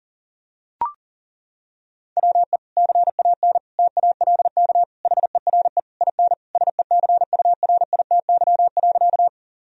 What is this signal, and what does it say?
Morse code, heard as a keyed tone, about 40 wpm, 700 Hz. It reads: WE CAN TALK HERE IN SECURITY.
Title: Sentences from Top 1000 Words - Review Part 4 - 40wpm